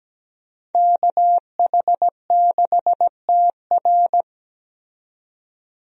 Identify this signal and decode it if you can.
Morse code, heard as a keyed tone, about 17 wpm, 700 Hz. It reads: KH6TR
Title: DXCC-100 Call Signs - 5 Characters - 17wpm